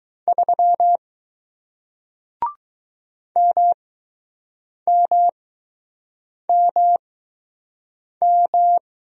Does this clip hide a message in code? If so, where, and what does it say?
Yes: Morse code from 0.3 to 8.8 s, heard as a keyed tone, about 20 wpm, spaced as for 6 wpm, 700 Hz. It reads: 3 MMMM